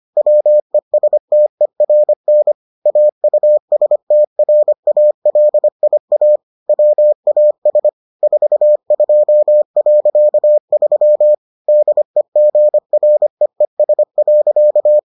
Morse code, heard as a keyed tone, about 25 wpm, 600 Hz. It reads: WESTERN AUSTRALIA WAS 42.3 DEGREES.